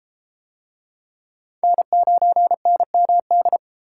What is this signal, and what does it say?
Morse code, heard as a keyed tone, about 30 wpm, 700 Hz. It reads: N9NMD